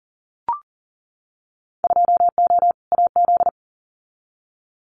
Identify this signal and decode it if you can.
Morse code, heard as a keyed tone, about 40 wpm, 700 Hz. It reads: 2O AZ
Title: Winter Field Day Exchanges 40wpm